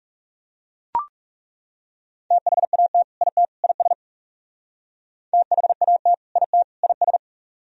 Morse code, heard as a keyed tone, about 37 wpm, 700 Hz. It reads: THAT IT IS THAT IT IS
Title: Speed-Racing - Sets of 3 Words 30wpm